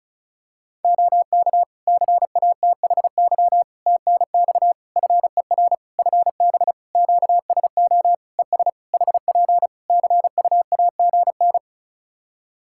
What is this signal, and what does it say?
Morse code, heard as a keyed tone, about 34 wpm, 700 Hz. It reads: OK CATHY TNX FER FB QSO ES HP CUAGN
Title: K1USN SST QSO Practice 35wpm